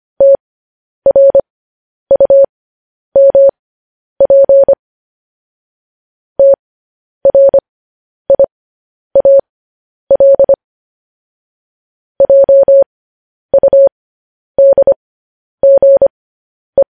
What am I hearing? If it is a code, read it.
Morse code, heard as a keyed tone, about 25 wpm, spaced as for 10 wpm, 550 Hz. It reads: TRUMP TRIAL JUDGE